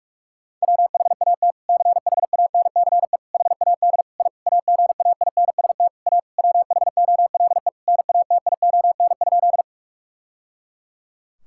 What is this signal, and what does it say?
Morse code, heard as a keyed tone, about 44 wpm, 700 Hz. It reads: WHAT CHANCE HAD I AGAINST A WHOLE NATION?
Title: Sentences from Top 1000 Words - Review Part 6 - 45wpm